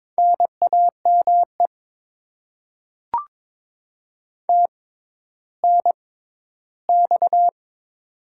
Morse code, heard as a keyed tone, about 21 wpm, spaced as for 12 wpm, 700 Hz. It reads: NAME TNX
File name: Top 20 QSO Elements - 6x Character Spacing - 22wpm